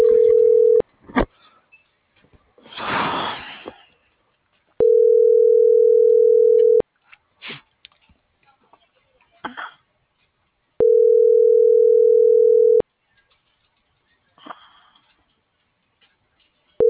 Ringback tone (North America), 440 + 480 Hz.